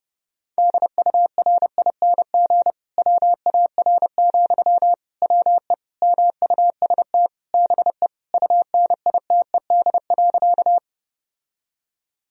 Morse code, heard as a keyed tone, about 32 wpm, 700 Hz. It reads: DURING WAR, WE MUST BE UNITED.
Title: Sentences from Top 1000 Words - Review Part 4 - 30wpm